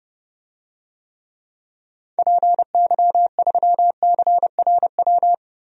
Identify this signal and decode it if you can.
Morse code, heard as a keyed tone, about 30 wpm, 700 Hz. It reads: PY3CRW
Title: DXCC-100 Call Signs - 6 Characters - 30wpm